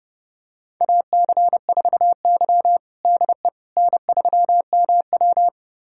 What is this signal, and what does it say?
Morse code, heard as a keyed tone, about 30 wpm, 700 Hz. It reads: AC4Y DE N3MW
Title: USA Callsign DE Callsign - 4 Characters - 30wpm